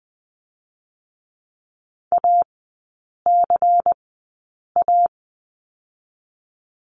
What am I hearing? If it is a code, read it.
Morse code, heard as a keyed tone, about 20 wpm, 700 Hz. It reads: A C A
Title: 218 Lesson 35e - Number 0 - Three Characters 20wpm